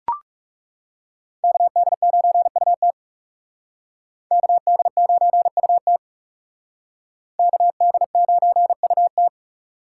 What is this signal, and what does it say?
Morse code, heard as a keyed tone, about 37 wpm, 700 Hz. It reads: KD9UT KD9UT KD9UT
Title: Speed Racing - USA 2x2 Call Signs 30wpm